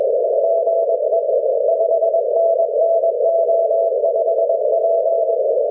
Morse code, heard as a keyed tone, about 21 wpm, 660 Hz, with noise. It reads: GE 5NN O5P
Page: JR7KGE (5W, sent 'P') on 7 MHz, JR7KGE andJH7FUJ (0.5W, sent 'Q') on 3.5 MHz, JR1UJX (0.5W) and JR7KGE on 1.9 MHz.